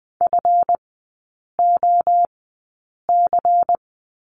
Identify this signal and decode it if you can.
Morse code, heard as a keyed tone, about 20 wpm, spaced as for 8 wpm, 700 Hz. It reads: FOC
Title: 138 Lesson 23e - Letter f - Three Characters 20wpm